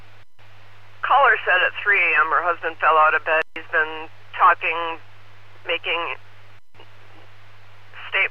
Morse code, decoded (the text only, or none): none